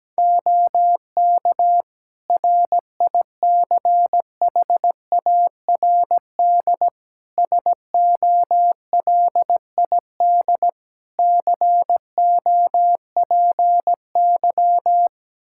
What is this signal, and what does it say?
Morse code, heard as a keyed tone, about 17 wpm, 700 Hz. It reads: OK RICHARD SOLID COPY
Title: Practice QSO - Exchange 3 - More Info - 17wpm